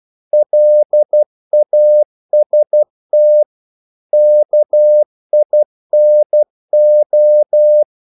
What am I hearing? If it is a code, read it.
Morse code, heard as a keyed tone, about 12 wpm, 600 Hz. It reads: LAST KINO
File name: Korte ord DK 12wpm